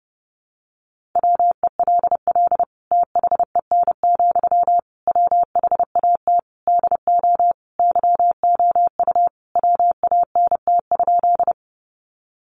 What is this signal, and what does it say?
Morse code, heard as a keyed tone, about 30 wpm, 700 Hz. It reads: WELL THEN, WHAT DO YOU WANT?
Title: Sentences from Top 200 Words 30wpm